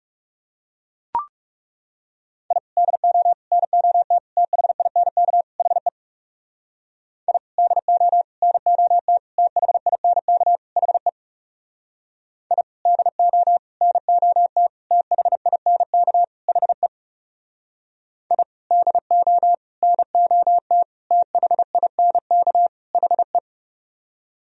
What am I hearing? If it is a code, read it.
Morse code, heard as a keyed tone, about 37 wpm, 700 Hz. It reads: I DO NOT THINK HE I DO NOT THINK HE I DO NOT THINK HE I DO NOT THINK HE
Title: Speed-Racing - Sets of 5 Words 30wpm